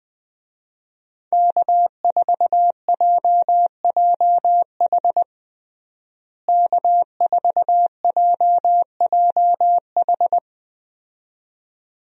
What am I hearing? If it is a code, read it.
Morse code, heard as a keyed tone, about 20 wpm, 700 Hz. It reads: K4JJH K4JJH
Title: USA Call Signs - 5 Characters - Repeated 2x - 20wpm